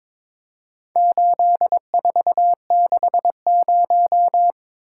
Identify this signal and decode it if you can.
Morse code, heard as a keyed tone, about 22 wpm, 700 Hz. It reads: 8460
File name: practice / 22wpm / Four Letter-Number Combinations - 22wpm